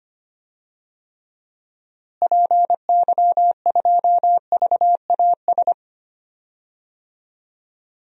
Morse code, heard as a keyed tone, about 25 wpm, 700 Hz. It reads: PY2VAS
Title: DXCC-100 Call Signs - 6 Characters - Repeated 2x - 25wpm